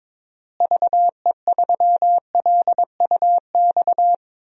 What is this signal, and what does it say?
Morse code, heard as a keyed tone, about 22 wpm, 700 Hz. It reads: VE3LUX
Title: DXCC-100 Call Signs - 6 Characters - Repeated 3x - 22wpm